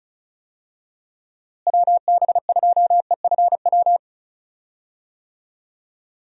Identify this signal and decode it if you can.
Morse code, heard as a keyed tone, about 36 wpm, 700 Hz. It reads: WB2EFW
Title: USA Call Signs - 6 Characters - Repeated 3x - 35wpm